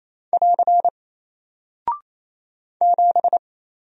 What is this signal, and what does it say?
Morse code, heard as a keyed tone, about 28 wpm, 700 Hz. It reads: <AR> 7
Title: Single Character (Letter:Number:Punctuation:Prosign) Alt 28wpm